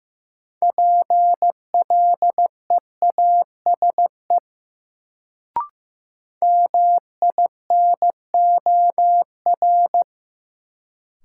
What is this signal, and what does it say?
Morse code, heard as a keyed tone, about 15 wpm, 700 Hz. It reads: PLEASE MINOR
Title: Sentences from Top 1500 Words - New Word Drill - 15wpm